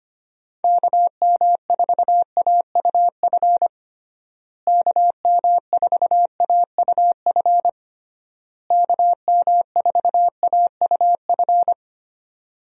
Morse code, heard as a keyed tone, about 25 wpm, 700 Hz. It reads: KM4AUF KM4AUF KM4AUF
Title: USA Call Signs - 6 Characters - Repeated 3x - 25wpm